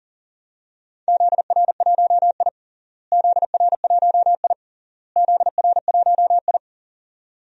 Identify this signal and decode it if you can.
Morse code, heard as a keyed tone, about 40 wpm, 700 Hz. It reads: ZR1I ZR1I ZR1I
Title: DXCC-100 Call Signs - 4 Characters - Repeated 3x - 40wpm